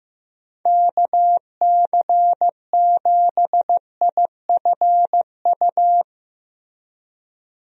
Morse code, heard as a keyed tone, about 15 wpm, 700 Hz. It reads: KC7IFU
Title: USA Call Signs - 6 Characters - 15wpm